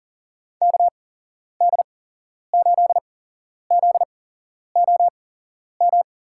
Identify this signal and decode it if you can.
Morse code, heard as a keyed tone, about 40 wpm, spaced as for 11 wpm, 700 Hz. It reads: KD8ZOM
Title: USA Call Signs - 8x Character Spacing - 40wpm